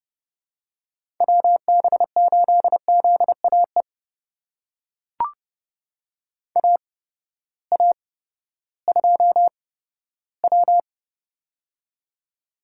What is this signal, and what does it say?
Morse code, heard as a keyed tone, about 30 wpm, spaced as for 15 wpm, 700 Hz. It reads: WB8ZAE AA2W
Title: USA Call Signs - 8x Character Spacing - 30wpm